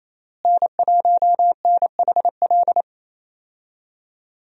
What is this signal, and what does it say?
Morse code, heard as a keyed tone, about 28 wpm, 700 Hz. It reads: N1NHL
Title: USA Call Signs - 5 Characters - Repeated 2x - 28wpm